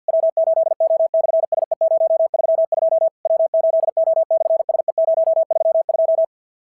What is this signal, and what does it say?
Morse code, heard as a keyed tone, about 49 wpm, 690 Hz. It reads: W8O/SE032 W8O/SE032
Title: SOTA QSO Practice 50wpm